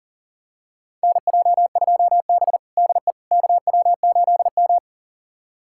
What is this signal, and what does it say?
Morse code, heard as a keyed tone, about 40 wpm, 700 Hz. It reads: NJ2B DE KW8M
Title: USA Callsign DE Callsign - 4 Characters - 40wpm